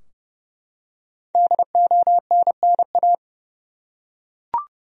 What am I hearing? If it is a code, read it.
Morse code, heard as a keyed tone, about 30 wpm, 700 Hz. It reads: DONNA
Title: USA Common Names - 30wpm